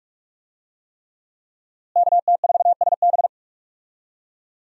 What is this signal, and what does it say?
Morse code, heard as a keyed tone, about 43 wpm, 700 Hz. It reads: KT4SB
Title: USA Call Signs - 5 Characters - Repeated 3x - 45wpm